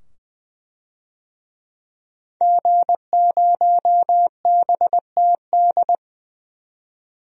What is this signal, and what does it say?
Morse code, heard as a keyed tone, about 20 wpm, 700 Hz. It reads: G0BTD
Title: DXCC-100 Call Signs - 5 Characters - 20wpm